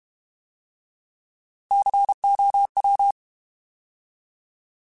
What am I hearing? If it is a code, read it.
Morse code, heard as a keyed tone, about 32 wpm, 800 Hz. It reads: COW
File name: Alphabetical / Alphabetical mp3s / Cs @30